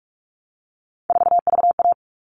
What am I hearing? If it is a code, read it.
Morse code, heard as a keyed tone, about 44 wpm, 700 Hz. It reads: 4VA
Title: All Cut Numbers - 3-Digit 45wpm